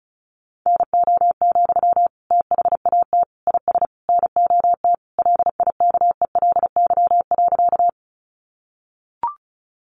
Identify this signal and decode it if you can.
Morse code, heard as a keyed tone, about 34 wpm, 700 Hz. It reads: NO, THAT IS NOT LIKELY.
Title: Sentences from Top 600 Words 35wpm